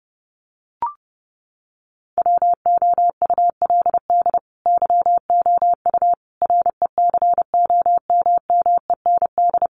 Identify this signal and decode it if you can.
Morse code, heard as a keyed tone, about 30 wpm, 700 Hz. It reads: WOULD YOU RECOMMEND
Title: Sentences from Top 1200 Words 30wpm